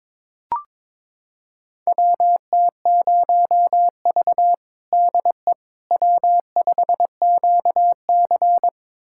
Morse code, heard as a keyed tone, about 22 wpm, 700 Hz. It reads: WT0V DE W5QC
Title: USA Callsign DE Callsign - 4 Characters - 22wpm